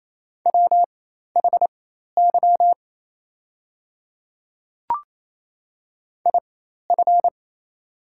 Morse code, heard as a keyed tone, about 28 wpm, spaced as for 13 wpm, 700 Hz. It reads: WHY IF
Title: 4x Character Spacing - Top 200 Words - 28wpm